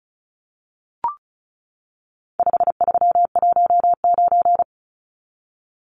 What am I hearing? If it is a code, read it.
Morse code, heard as a keyed tone, about 34 wpm, 700 Hz. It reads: 5319
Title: Numbers - 4-Digit 35wpm